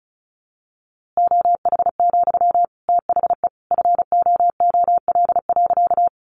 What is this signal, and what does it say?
Morse code, heard as a keyed tone, about 34 wpm, 700 Hz. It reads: OH, THE FOOL.
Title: A Sound Of Thunder 35wpm